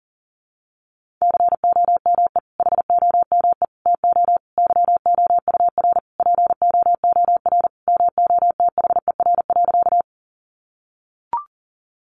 Morse code, heard as a keyed tone, about 40 wpm, 700 Hz. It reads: COME HOME TO YOUR POOR MOTHER.
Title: Sentences from Top 1000 Words 40wpm